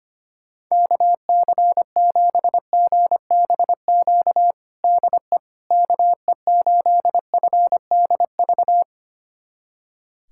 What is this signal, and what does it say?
Morse code, heard as a keyed tone, about 25 wpm, 700 Hz. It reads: KC7GBQ DE KE8FDV
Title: USA Callsign DE Callsign - 6 Characters - 25wpm